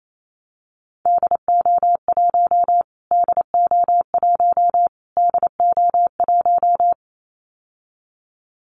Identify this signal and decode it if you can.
Morse code, heard as a keyed tone, about 28 wpm, 700 Hz. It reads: DO1 DO1 DO1
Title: Callsign Prefixes for Top 25 DXCC - Repeated 3x - 28wpm